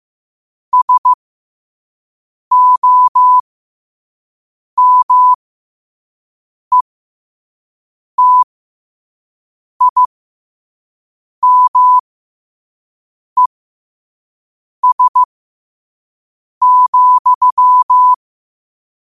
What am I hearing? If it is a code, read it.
Morse code, heard as a keyed tone, about 15 wpm, spaced as for 5 wpm, 1000 Hz. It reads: SOMETIMES,
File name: qotdfiles / 5wpm